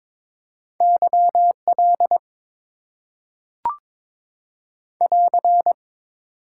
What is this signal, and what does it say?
Morse code, heard as a keyed tone, about 22 wpm, 700 Hz. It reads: YL <AR>
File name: Top 90 QSO Elements - 8x Character Spacing - 22wpm